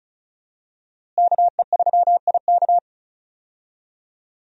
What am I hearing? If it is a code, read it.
Morse code, heard as a keyed tone, about 36 wpm, 700 Hz. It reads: KE3IK
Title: USA Call Signs - 5 Characters - Repeated 3x - 35wpm